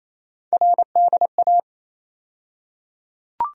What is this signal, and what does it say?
Morse code, heard as a keyed tone, about 28 wpm, 700 Hz. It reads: RDA
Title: USA Call Sign Postfixes - 3 Letters 28wpm